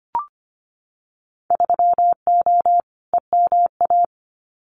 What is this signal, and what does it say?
Morse code, heard as a keyed tone, about 25 wpm, 690 Hz. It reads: 3O EMA
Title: Winter Field Day Exchanges 25wpm